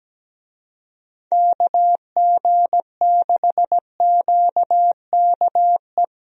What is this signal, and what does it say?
Morse code, heard as a keyed tone, about 17 wpm, 700 Hz. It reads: KG6QKE
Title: USA Call Signs - 6 Characters - Repeated 2x - 17wpm